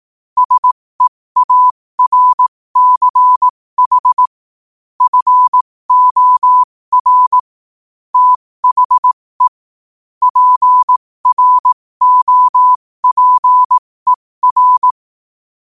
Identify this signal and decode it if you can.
Morse code, heard as a keyed tone, about 18 wpm, spaced as for 15 wpm, 1000 Hz. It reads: SEARCH FOR THE PROPER
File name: qotdfiles / 15wpm